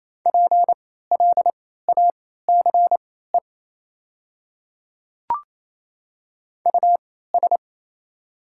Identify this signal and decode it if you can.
Morse code, heard as a keyed tone, about 27 wpm, spaced as for 16 wpm, 690 Hz. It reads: PLACE US
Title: 3x Character Spacing - Top 200 Words - 28wpm